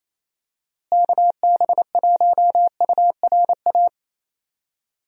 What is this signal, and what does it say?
Morse code, heard as a keyed tone, about 28 wpm, 700 Hz. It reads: KB1URA
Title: USA Call Signs - 6 Characters - Repeated 3x - 28wpm